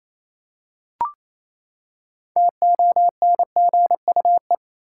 Morse code, heard as a keyed tone, about 28 wpm, 700 Hz. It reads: TONGUE
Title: Six Letter Words - 28wpm